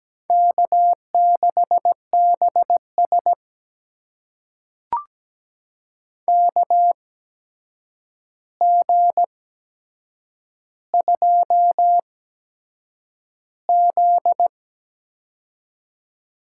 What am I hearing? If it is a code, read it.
Morse code, heard as a keyed tone, about 17 wpm, spaced as for 7 wpm, 700 Hz. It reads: K6BS KG2Z
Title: USA Call Signs - 4 Characters - 8x Character Spacing - 17wpm